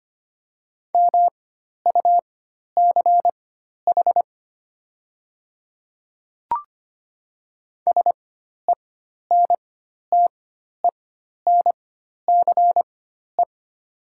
Morse code, heard as a keyed tone, about 25 wpm, spaced as for 12 wpm, 700 Hz. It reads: MUCH SENTENCE
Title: 4x Character Spacing - Top 200 Words - 25wpm